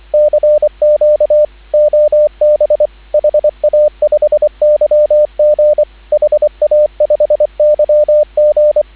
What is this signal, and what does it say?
Morse code, heard as a keyed tone, about 24 wpm, 600 Hz, with radio noise. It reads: CQ OB HA5YG HA5YG